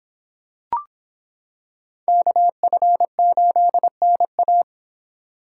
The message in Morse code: KF8NA

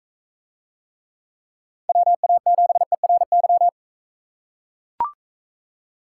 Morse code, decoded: WA7ERY

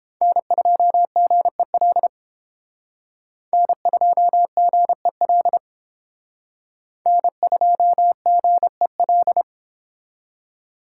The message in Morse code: N2GEL N2GEL N2GEL

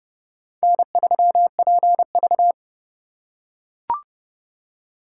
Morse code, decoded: N3PV

